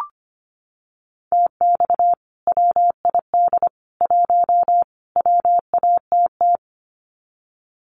Morse code, heard as a keyed tone, about 25 wpm, 700 Hz. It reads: TX WID 1 WATT
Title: Common QSO Phrases 25wpm